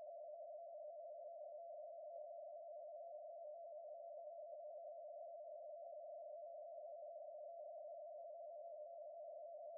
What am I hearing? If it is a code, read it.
Two-tone teleprinter signal, about 80 Hz shift.